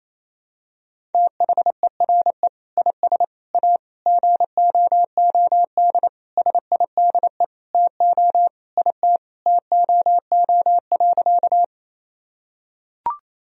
Morse code, heard as a keyed tone, about 27 wpm, 700 Hz. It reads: THERE IS A GOOD SIDE TO IT TOO.
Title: Sentences from Top 1000 Words - Review Part 1 - 28wpm